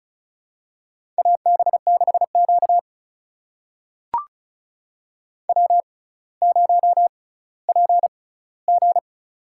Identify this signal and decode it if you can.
Morse code, heard as a keyed tone, about 36 wpm, spaced as for 18 wpm, 700 Hz. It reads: AB6Q W0PG
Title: USA Call Signs - 4 Characters - 6x Character Spacing - 35wpm